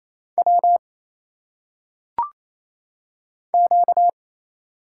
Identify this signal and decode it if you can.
Morse code, heard as a keyed tone, about 27 wpm, 700 Hz. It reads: W Q
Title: Single Letters - Phonetic Pronunciation 28wpm